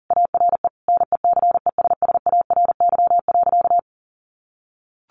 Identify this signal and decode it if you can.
Morse code, heard as a keyed tone, about 40 wpm, 700 Hz. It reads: ARE NECESSARY.
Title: Sentences from Top 1400 Words 40wpm